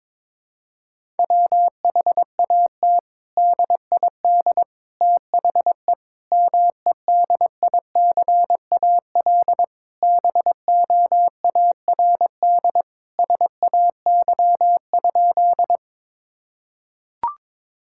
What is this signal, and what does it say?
Morse code, heard as a keyed tone, about 21 wpm, 700 Hz. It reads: WHAT DID THE MEDICAL BOARD SAY?
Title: Sentences from Top 1000 Words - Review Part 5 - 22wpm